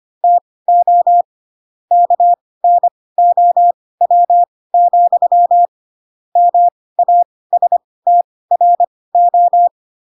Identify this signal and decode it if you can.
Morse code, heard as a keyed tone, about 25 wpm, spaced as for 18 wpm, 700 Hz. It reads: TO KNOW, MASTRO
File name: Pinocchio / Pinocchio ch2 12 25 ik1hge